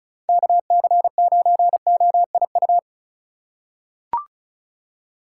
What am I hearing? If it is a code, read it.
Morse code, heard as a keyed tone, about 36 wpm, 690 Hz. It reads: KC9OIU
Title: USA Call Signs - 6 Characters - 3x Character Spacing - 35wpm